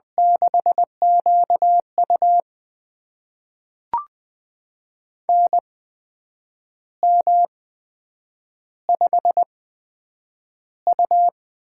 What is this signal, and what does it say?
Morse code, heard as a keyed tone, about 20 wpm, spaced as for 8 wpm, 700 Hz. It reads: 6QU NM5U
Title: USA Call Signs - 4 Characters - 8x Character Spacing - 20wpm